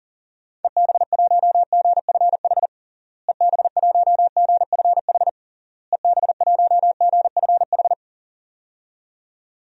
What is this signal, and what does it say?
Morse code, heard as a keyed tone, about 40 wpm, 700 Hz. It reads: EB1GFH EB1GFH EB1GFH
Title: DXCC-100 Call Signs - 6 Characters - Repeated 3x - 40wpm